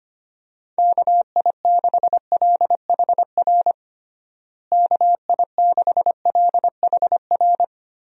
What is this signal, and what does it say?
Morse code, heard as a keyed tone, about 25 wpm, 700 Hz. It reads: KI6LHR KI6LHR
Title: USA Call Signs - 6 Characters - Repeated 3x - 25wpm